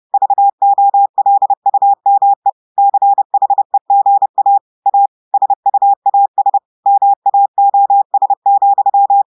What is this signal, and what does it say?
Morse code, heard as a keyed tone, about 30 wpm, 800 Hz. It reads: VOLUME CHEGA A SUAS MAOS,
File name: audios / portugais / portugais texte en clair 1 vitesse 30 mots minutes